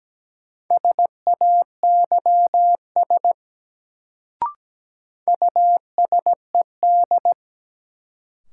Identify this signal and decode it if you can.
Morse code, heard as a keyed tone, about 17 wpm, 700 Hz. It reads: SAYS USED
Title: Sets of 2 Words - New Word Drill - 17wpm